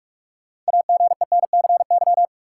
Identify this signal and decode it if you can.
Morse code, heard as a keyed tone, about 46 wpm, 700 Hz. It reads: AGENCY
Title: Sentences from Top 1300 Words - New Word Drill Alt - 45wpm